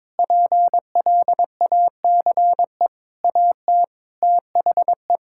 Morse code, heard as a keyed tone, about 22 wpm, 700 Hz. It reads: PLACE AT THE
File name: practice / 22wpm / Sets of 8 Words - 22wpm